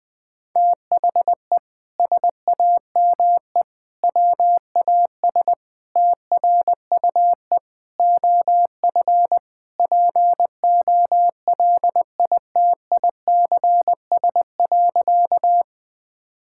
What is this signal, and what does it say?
Morse code, heard as a keyed tone, about 20 wpm, 700 Hz. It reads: THE SAME WAS TRUE OF POLITICS.
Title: Sentences from Top 1000 Words - Review Part 6 - 20wpm